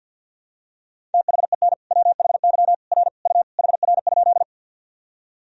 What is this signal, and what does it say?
Morse code, heard as a keyed tone, about 51 wpm, 700 Hz. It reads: THEN WHY R U HR?
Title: Sentences from Top 200 Words and Abbreviations 50wpm